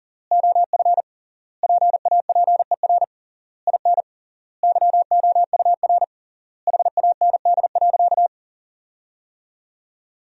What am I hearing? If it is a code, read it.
Morse code, heard as a keyed tone, about 40 wpm, 700 Hz. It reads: OF PAPER IN YOUR HAND.
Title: 3x Word-Spacing - Sentences from Top 400 Words - 40wpm